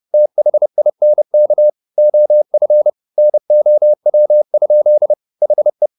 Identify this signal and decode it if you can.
Morse code, heard as a keyed tone, about 30 wpm, 600 Hz. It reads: THINK OF NOW? HE